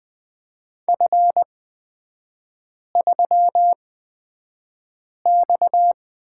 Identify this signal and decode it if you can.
Morse code, heard as a keyed tone, about 20 wpm, spaced as for 5 wpm, 700 Hz. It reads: F3X
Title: 229 Lesson 37e - Letter x - Three Characters 20wpm